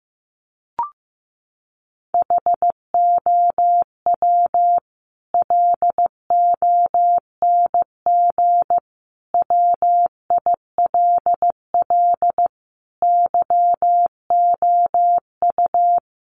Morse code, heard as a keyed tone, about 15 wpm, 700 Hz. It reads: HOW LONG WILL YOU
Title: Sentences from Top 1200 Words 15wpm